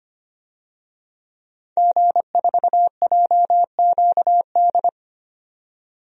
Morse code, heard as a keyed tone, about 25 wpm, 700 Hz. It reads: G4JQD
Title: DXCC-100 Call Signs - 5 Characters - 25wpm